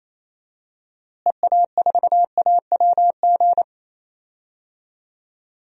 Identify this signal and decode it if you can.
Morse code, heard as a keyed tone, about 28 wpm, 700 Hz. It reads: EA4AWG
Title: DXCC-100 Call Signs - 6 Characters - Repeated 3x - 28wpm